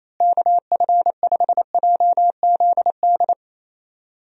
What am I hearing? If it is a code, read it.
Morse code, heard as a keyed tone, about 28 wpm, 700 Hz. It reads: KF5JZD
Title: USA Call Signs - 6 Characters - Repeated 2x - 28wpm